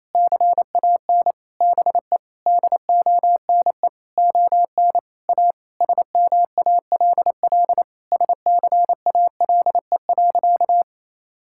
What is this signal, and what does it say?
Morse code, heard as a keyed tone, about 28 wpm, 700 Hz. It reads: CAN BE DONE ON A SMALL SCALE.
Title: Sentences from Top 1000 Words - Review Part 5 - 28wpm